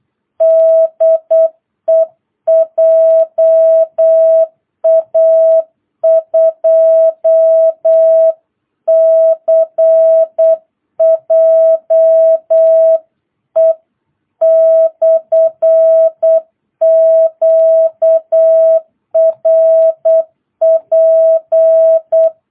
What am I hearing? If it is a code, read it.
Morse code, heard as a keyed tone, about 8 wpm, 660 Hz. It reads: DE JA2CJE/QRP